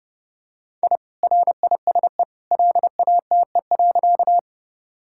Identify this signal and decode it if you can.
Morse code, heard as a keyed tone, about 30 wpm, 700 Hz. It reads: I RISE LATE.
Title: Sentences from Top 1000 Words - Review Part 6 - 30wpm